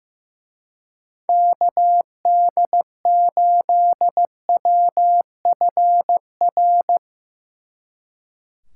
Morse code, heard as a keyed tone, about 15 wpm, 700 Hz. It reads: KD8WFR
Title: Speed Racing - USA 2x3 Call Signs 15wpm